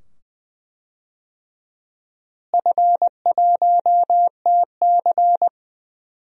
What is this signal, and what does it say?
Morse code, heard as a keyed tone, about 20 wpm, 700 Hz. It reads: F1TC
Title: DXCC-100 Call Signs - 4 Characters - Repeated 2x - 20wpm